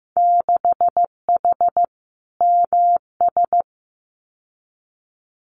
Morse code, heard as a keyed tone, about 15 wpm, 700 Hz. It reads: 6H MS
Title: Winter Field Day Exchanges Encore 15wpm